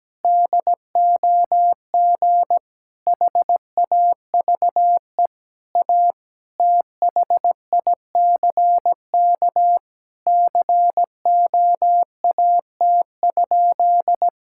Morse code, heard as a keyed tone, about 17 wpm, 700 Hz. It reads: DOG HAVE A THICK COAT?
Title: Sentences from Top 1600 Words 17wpm